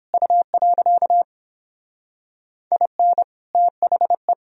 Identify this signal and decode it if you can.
Morse code, heard as a keyed tone, about 31 wpm, 700 Hz. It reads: U. IN THE